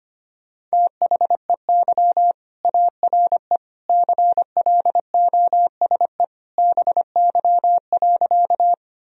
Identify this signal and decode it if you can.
Morse code, heard as a keyed tone, about 25 wpm, 700 Hz. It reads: THEY ARE CLOSE BY.
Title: Sentences from Top 1000 Words - Review Part 1 - 25wpm